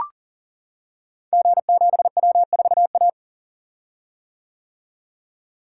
Morse code, heard as a keyed tone, about 39 wpm, 700 Hz. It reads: G7W4A
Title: Code Groups - 5 Letter:Numbers - 40wpm